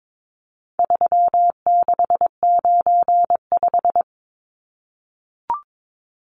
Morse code, heard as a keyed tone, about 22 wpm, 700 Hz. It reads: 3695
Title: Numbers - 4-Digit 22wpm